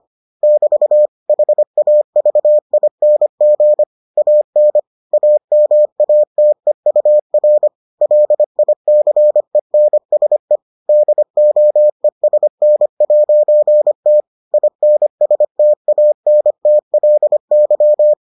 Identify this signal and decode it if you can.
Morse code, heard as a keyed tone, about 25 wpm, 600 Hz. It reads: <BT> HAVING AN AMATEUR LICENSE DOESN'T INSTANTLY